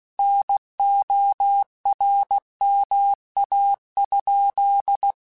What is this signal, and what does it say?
Morse code, heard as a keyed tone, about 16 wpm, 800 Hz. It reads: NORMA?
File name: QSOs / QSOs-mp3s / WD0FBT 1@15